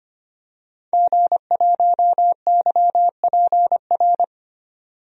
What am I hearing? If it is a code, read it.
Morse code, heard as a keyed tone, about 25 wpm, 700 Hz. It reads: G1YPR
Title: DXCC-100 Call Signs - 5 Characters - 25wpm